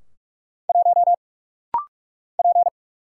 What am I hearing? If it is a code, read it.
Morse code, heard as a keyed tone, about 44 wpm, 700 Hz. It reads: 1 P
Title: Single Letter-Number - Rapid-Fire 45wpm